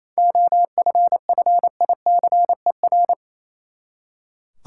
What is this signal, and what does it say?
Morse code, heard as a keyed tone, about 28 wpm, 700 Hz. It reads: OFFICER
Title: Seven Letter Words - 28wpm